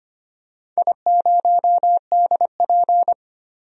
Morse code, heard as a keyed tone, about 25 wpm, 700 Hz. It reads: I0DP
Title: DXCC-100 Call Signs - 4 Characters - 25wpm